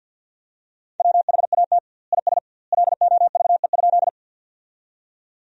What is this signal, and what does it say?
Morse code, heard as a keyed tone, about 50 wpm, 700 Hz. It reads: WHAT IS LOVE?
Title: 2x Word-Spacing - Sentences from Top 400 Words - 50wpm